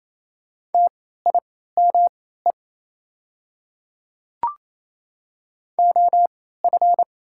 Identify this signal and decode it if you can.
Morse code, heard as a keyed tone, about 28 wpm, spaced as for 16 wpm, 690 Hz. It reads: TIME OF
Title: 3x Character Spacing - Top 100 Words - 28wpm